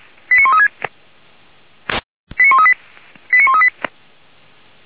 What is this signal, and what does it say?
Five-tone selective call (ZVEI), heard as quick runs of five tones, tones of about 80 ms. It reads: 89136 89138 89138